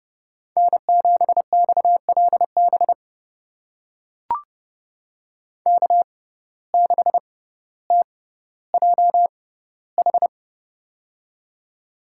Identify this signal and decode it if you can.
Morse code, heard as a keyed tone, about 30 wpm, spaced as for 15 wpm, 700 Hz. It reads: N7XLB K6TJH
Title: USA Call Signs - 5 Characters - 6x Character Spacing - 30wpm